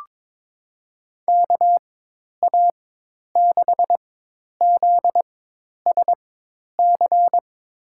Morse code, heard as a keyed tone, about 21 wpm, spaced as for 10 wpm, 700 Hz. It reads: KA6ZSC
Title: USA Call Signs - 6 Characters - 4x Character Spacing - 22wpm